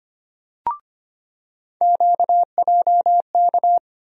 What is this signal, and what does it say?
Morse code, heard as a keyed tone, about 25 wpm, 700 Hz. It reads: QJK
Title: USA Call Sign Postfixes - 3 Letters 25wpm